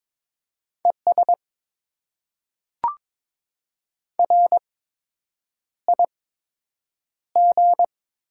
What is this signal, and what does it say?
Morse code, heard as a keyed tone, about 22 wpm, spaced as for 8 wpm, 700 Hz. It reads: ES RIG
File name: Top 90 QSO Elements - 8x Character Spacing - 22wpm